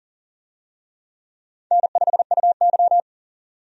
Morse code, heard as a keyed tone, about 40 wpm, 700 Hz. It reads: N5UY
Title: USA Call Signs - 4 Characters - 40wpm